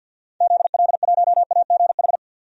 Morse code, heard as a keyed tone, about 50 wpm, 700 Hz. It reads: ZL1AGH
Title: DXCC-100 Call Signs - 6 Characters - 50wpm